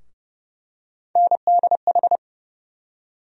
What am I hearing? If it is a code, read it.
Morse code, heard as a keyed tone, about 30 wpm, 700 Hz. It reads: NDH
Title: USA Call Sign Postfixes - 3 Letters 30wpm